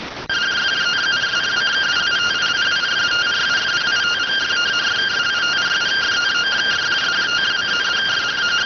Two-tone teleprinter signal, about 170 Hz shift.